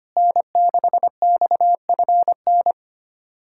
Morse code, heard as a keyed tone, about 25 wpm, 700 Hz. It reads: N6XFN
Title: USA Callsign DE Callsign 25wpm